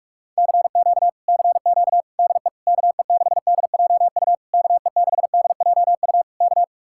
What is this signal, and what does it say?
Morse code, heard as a keyed tone, about 59 wpm, spaced as for 51 wpm, 700 Hz. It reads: CQ CQ DE KE6DJU KE6DJU K
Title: Speed-Racing Common QSO Phrases Encore - 45wpm